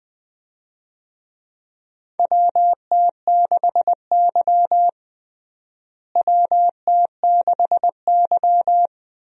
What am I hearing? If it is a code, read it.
Morse code, heard as a keyed tone, about 20 wpm, 700 Hz. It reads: WT6Y WT6Y
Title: USA Call Signs - 4 Characters - Repeated 2x - 20wpm